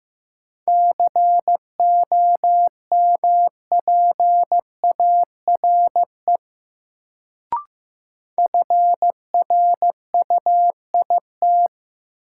Morse code, heard as a keyed tone, about 15 wpm, 700 Hz. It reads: COMPARE FRUIT
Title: Sentences from Top 1300 Words - New Word Drill Alt - 15wpm